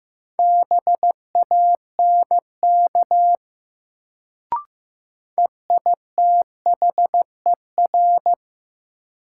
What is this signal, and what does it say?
Morse code, heard as a keyed tone, about 15 wpm, 700 Hz. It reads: BANK EITHER
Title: Sets of 5 Words - New Word Drill - 15wpm